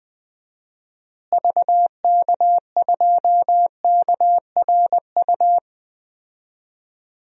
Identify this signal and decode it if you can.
Morse code, heard as a keyed tone, about 20 wpm, 700 Hz. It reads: VK2KRU
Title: DXCC-100 Call Signs - 6 Characters - Repeated 2x - 20wpm